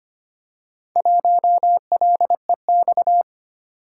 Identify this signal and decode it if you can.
Morse code, heard as a keyed tone, about 25 wpm, 700 Hz. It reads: 1LEX
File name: practice / 25wpm / Four Letter-Number Combinations - 25wpm